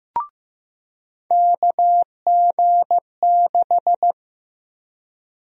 Morse code, heard as a keyed tone, about 15 wpm, 700 Hz. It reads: KG6
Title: USA Call Sign Prefixes - 15wpm